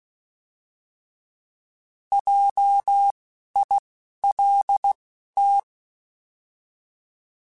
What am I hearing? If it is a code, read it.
Morse code, heard as a keyed tone, about 16 wpm, 800 Hz. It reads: J I L T